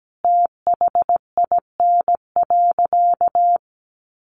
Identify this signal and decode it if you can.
Morse code, heard as a keyed tone, about 17 wpm, 700 Hz. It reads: THIN.
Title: Sentences from Top 1200 Words 17wpm